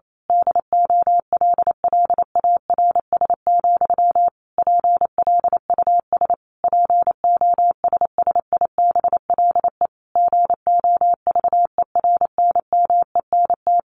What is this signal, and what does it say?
Morse code, heard as a keyed tone, about 28 wpm, 700 Hz. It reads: DOLLARS, PLUS POSSIBLE GOVERNMENT
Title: A Sound Of Thunder 28wpm